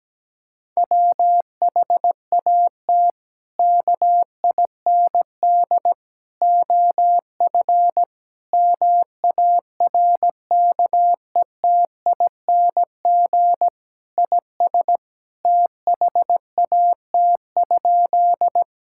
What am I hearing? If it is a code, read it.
Morse code, heard as a keyed tone, about 17 wpm, 700 Hz. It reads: WHAT KIND OF MARKETING IS THAT?
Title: Sentences from Top 1000 Words - Review Part 5 - 17wpm